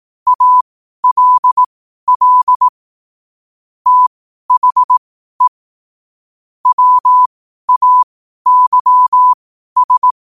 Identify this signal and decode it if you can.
Morse code, heard as a keyed tone, about 18 wpm, spaced as for 12 wpm, 1000 Hz. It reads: ALL THE WAYS
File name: qotdfiles / 12wpm